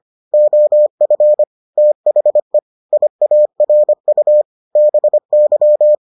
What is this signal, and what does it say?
Morse code, heard as a keyed tone, about 25 wpm, 600 Hz. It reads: OF THE IARU BY